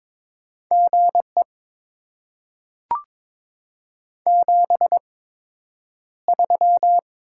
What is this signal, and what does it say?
Morse code, heard as a keyed tone, about 22 wpm, spaced as for 9 wpm, 700 Hz. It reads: GE 73
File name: Top 20 QSO Elements - 8x Character Spacing - 22wpm